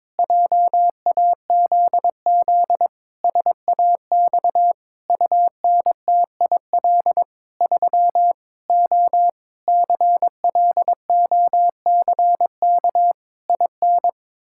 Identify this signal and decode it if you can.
Morse code, heard as a keyed tone, about 22 wpm, 700 Hz. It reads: JAZZ SAX UNTIL 3 O CLOCK IN
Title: Pangram Challenge v2 - 22wpm